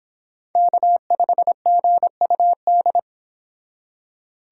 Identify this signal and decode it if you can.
Morse code, heard as a keyed tone, about 26 wpm, 700 Hz. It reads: K5GUD